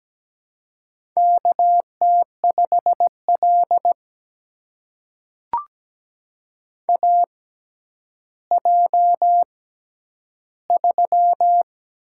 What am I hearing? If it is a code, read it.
Morse code, heard as a keyed tone, about 17 wpm, spaced as for 10 wpm, 700 Hz. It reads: KT5L AJ3
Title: USA Call Signs - 4 Characters - 6x Character Spacing - 17wpm